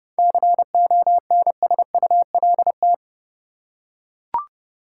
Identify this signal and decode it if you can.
Morse code, heard as a keyed tone, about 32 wpm, 700 Hz. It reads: CONSULT
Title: Seven Letter Words - 30wpm